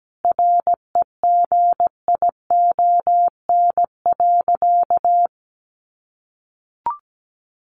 Morse code, heard as a keyed tone, about 17 wpm, 700 Hz. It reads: REGION.
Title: Sentences from Top 1500 Words 17wpm